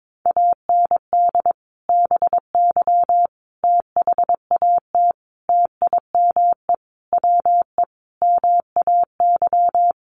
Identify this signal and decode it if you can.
Morse code, heard as a keyed tone, about 22 wpm, 700 Hz. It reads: AND BY THAT TIME WE MAY
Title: Sentences from Top 1100 Words 22wpm